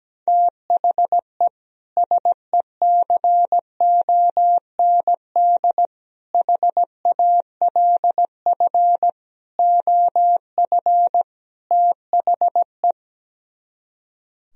Morse code, heard as a keyed tone, about 17 wpm, 700 Hz. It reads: THE SECOND HALF OF THE
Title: Speed-Racing - Sets of 5 Words 17wpm